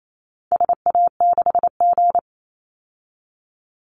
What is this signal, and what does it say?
Morse code, heard as a keyed tone, about 28 wpm, 700 Hz. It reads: SA6G
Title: Call Signs Repeated 2x - 28wpm